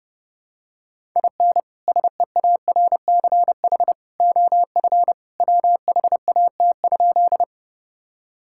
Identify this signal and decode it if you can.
Morse code, heard as a keyed tone, about 30 wpm, 700 Hz. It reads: IN SEARCH OF WHAT?
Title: Sentences from Top 1000 Words - Review Part 5 - 30wpm